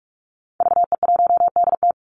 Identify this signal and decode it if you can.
Morse code, heard as a keyed tone, about 44 wpm, 700 Hz. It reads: VE1DT
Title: Call Signs Repeated 3x - 45wpm